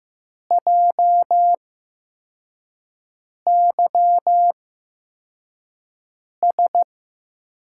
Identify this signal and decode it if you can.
Morse code, heard as a keyed tone, about 15 wpm, spaced as for 4 wpm, 700 Hz. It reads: JYS
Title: USA Call Signs - 6 Characters - 8x Character Spacing - 15wpm